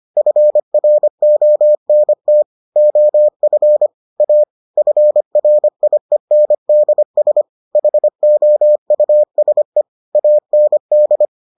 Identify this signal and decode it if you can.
Morse code, heard as a keyed tone, about 25 wpm, 600 Hz. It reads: FRONT OF A FRIENDS HOUSE AND